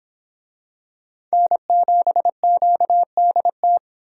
Morse code, heard as a keyed tone, about 26 wpm, 700 Hz. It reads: N7QDT